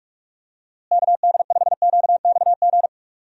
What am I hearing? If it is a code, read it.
Morse code, heard as a keyed tone, about 46 wpm, 700 Hz. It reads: KD5QXG
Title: USA Call Signs - 6 Characters - Repeated 3x - 45wpm